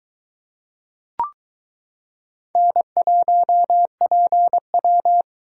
Morse code, heard as a keyed tone, about 22 wpm, 700 Hz. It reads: N1PW